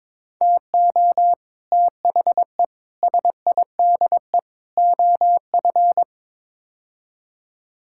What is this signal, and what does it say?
Morse code, heard as a keyed tone, about 22 wpm, 700 Hz. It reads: TO THE SIDE OF
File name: practice / 22wpm / Build A Phrase - 5-Word Phrases - 22wpm